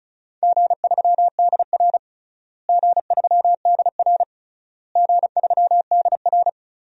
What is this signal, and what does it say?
Morse code, heard as a keyed tone, about 36 wpm, 700 Hz. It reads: G3DR G3DR G3DR
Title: DXCC-100 Call Signs - 4 Characters - Repeated 3x - 35wpm